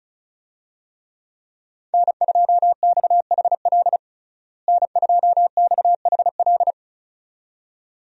Morse code, heard as a keyed tone, about 36 wpm, 700 Hz. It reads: N2XHL N2XHL
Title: USA Call Signs - 5 Characters - Repeated 2x - 35wpm